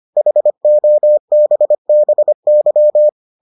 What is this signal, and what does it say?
Morse code, heard as a keyed tone, about 25 wpm, 600 Hz. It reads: HOBBY